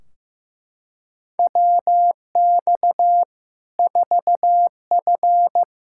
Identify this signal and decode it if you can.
Morse code, heard as a keyed tone, about 16 wpm, 700 Hz. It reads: WX 4F
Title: Practice QSO - Exchange 2 - Info - 15wpm